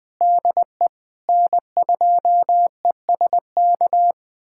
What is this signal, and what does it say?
Morse code, heard as a keyed tone, about 21 wpm, 700 Hz. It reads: DE N2ESK
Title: USA Callsign DE Callsign - 5 Characters - 20wpm